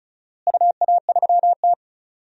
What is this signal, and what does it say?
Morse code, heard as a keyed tone, about 34 wpm, 700 Hz. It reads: UA3T
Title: DXCC-100 Call Signs - 4 Characters - Repeated 3x - 35wpm